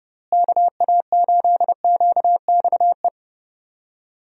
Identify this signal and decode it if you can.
Morse code, heard as a keyed tone, about 30 wpm, 700 Hz. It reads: KA8QXE
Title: USA Call Signs - 6 Characters - Repeated 3x - 30wpm